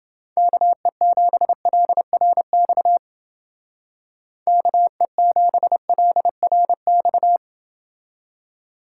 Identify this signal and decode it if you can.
Morse code, heard as a keyed tone, about 28 wpm, 700 Hz. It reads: KE7LRX KE7LRX